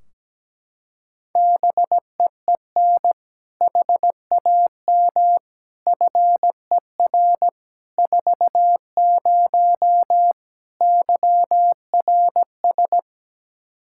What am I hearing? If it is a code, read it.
Morse code, heard as a keyed tone, about 17 wpm, 700 Hz. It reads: BEEN HAM FER 40 YRS
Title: Practice QSO - Exchange 3 - More Info - 17wpm